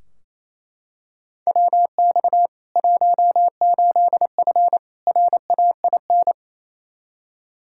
Morse code, heard as a keyed tone, about 28 wpm, 700 Hz. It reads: WX 18F RAIN
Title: Practice QSO - Exchange 2 - Info - 28wpm